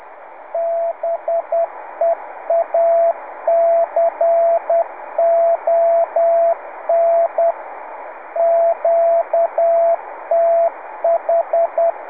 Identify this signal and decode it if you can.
Morse code, heard as a keyed tone, about 10 wpm, 670 Hz. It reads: BEACON QTH
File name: on1alj / la7uhf.70cm. .2248